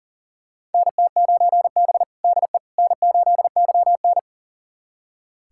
Morse code, heard as a keyed tone, about 37 wpm, 700 Hz. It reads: NT9B DE N8YN
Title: USA Callsign DE Callsign - 4 Characters - 40wpm